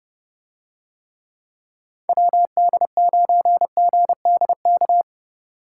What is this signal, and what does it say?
Morse code, heard as a keyed tone, about 29 wpm, 700 Hz. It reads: WD9GDK